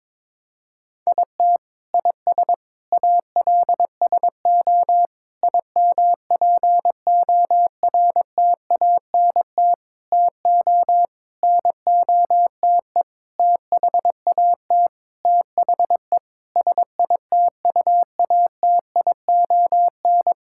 Morse code, heard as a keyed tone, about 22 wpm, 700 Hz. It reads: IT IS ALSO IMPORTANT TO NOTE THAT THE SITUATION
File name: practice / 22wpm / Sentences with 8-Word Phrases - 22wpm